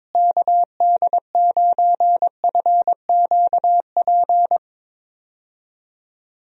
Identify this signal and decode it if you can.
Morse code, heard as a keyed tone, about 22 wpm, 700 Hz. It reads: KD9FQP
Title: USA Call Signs - 6 Characters - Repeated 2x - 22wpm